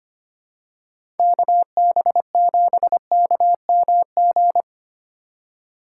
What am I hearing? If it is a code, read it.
Morse code, heard as a keyed tone, about 25 wpm, 700 Hz. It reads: KB7KMG
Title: USA Call Signs - 6 Characters - 4x Character Spacing - 25wpm